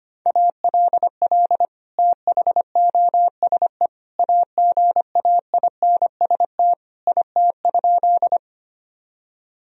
Morse code, heard as a keyed tone, about 25 wpm, 700 Hz. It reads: ALL THOSE AGAINST IT?
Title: Sentences from Top 1000 Words - Review Part 1 - 25wpm